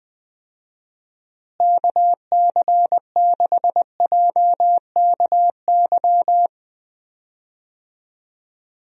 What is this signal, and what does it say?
Morse code, heard as a keyed tone, about 20 wpm, 700 Hz. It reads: KC6JKY
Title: USA Call Signs - 6 Characters - Repeated 2x - 20wpm